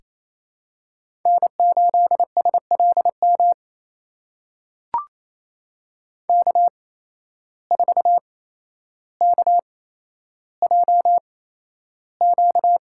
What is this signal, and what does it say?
Morse code, heard as a keyed tone, about 28 wpm, spaced as for 12 wpm, 700 Hz. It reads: N8SLM K4KJQ
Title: USA Call Signs - 5 Characters - 8x Character Spacing - 28wpm